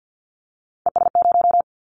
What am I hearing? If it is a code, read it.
Morse code, heard as a keyed tone, about 50 wpm, 700 Hz. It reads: ES0
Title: Callsign Prefixes for Top 50 DXCC - 50wpm